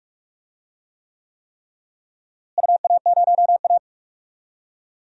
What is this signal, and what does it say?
Morse code, heard as a keyed tone, about 46 wpm, 700 Hz. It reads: UA0A
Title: DXCC-100 Call Signs - 4 Characters - Repeated 3x - 45wpm